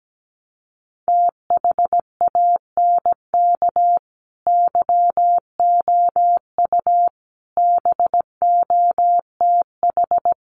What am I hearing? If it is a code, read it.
Morse code, heard as a keyed tone, about 17 wpm, 700 Hz. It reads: THANK YOU BOTH
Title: Sets of 5 Words 17wpm